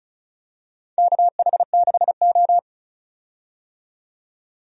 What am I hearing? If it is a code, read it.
Morse code, heard as a keyed tone, about 36 wpm, 700 Hz. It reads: KH6O
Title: DXCC-100 Call Signs - 4 Characters - 35wpm